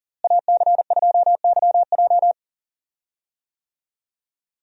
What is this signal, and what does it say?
Morse code, heard as a keyed tone, about 40 wpm, 700 Hz. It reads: AC2YJ
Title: USA Call Signs - 5 Characters - Repeated 2x - 40wpm